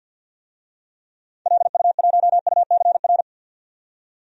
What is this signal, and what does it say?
Morse code, heard as a keyed tone, about 46 wpm, 700 Hz. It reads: LU1UCR
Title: DXCC-100 Call Signs - 6 Characters - 50wpm